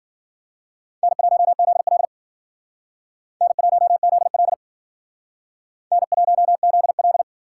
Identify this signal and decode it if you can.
Morse code, heard as a keyed tone, about 52 wpm, 700 Hz. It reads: N1ZL N1ZL N1ZL